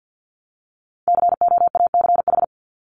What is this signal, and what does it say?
Morse code, heard as a keyed tone, about 50 wpm, 700 Hz. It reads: COACH
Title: Top 1751-2000 Words 50wpm